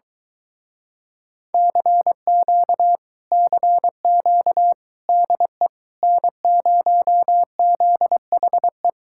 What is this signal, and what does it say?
Morse code, heard as a keyed tone, about 23 wpm, 700 Hz. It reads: CQ CQ DE N0ZHE